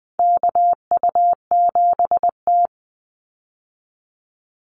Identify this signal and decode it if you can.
Morse code, heard as a keyed tone, about 20 wpm, 700 Hz. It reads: KU7T
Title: Call Signs Repeated 3x - 20wpm